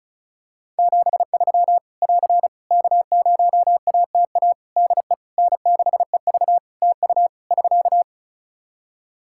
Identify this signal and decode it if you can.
Morse code, heard as a keyed tone, about 34 wpm, 700 Hz. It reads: 73 <AR> K0ATA DE N6EV TU <SK>
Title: Practice QSO - Exchange 4 - Ending - 35wpm